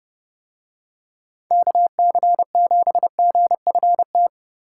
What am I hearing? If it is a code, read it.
Morse code, heard as a keyed tone, about 30 wpm, 700 Hz. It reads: KC7GFT